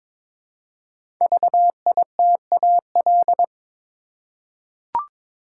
Morse code, heard as a keyed tone, about 23 wpm, 700 Hz. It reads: VITAL
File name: 5 Letter Words - First 3 Letters Given - 22wpm